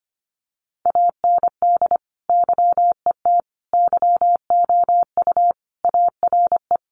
Morse code, heard as a keyed tone, about 25 wpm, 700 Hz. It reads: AND YET YOU ARE
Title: Sentences from Top 500 Words 25wpm